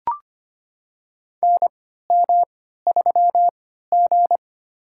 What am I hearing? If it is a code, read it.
Morse code, heard as a keyed tone, about 25 wpm, spaced as for 14 wpm, 700 Hz. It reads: NM3G
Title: USA Call Signs - 4 Characters - 3x Character Spacing - 25wpm